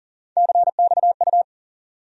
Morse code, heard as a keyed tone, about 40 wpm, 700 Hz. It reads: CXU
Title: USA Call Sign Postfixes - 3 Letters 40wpm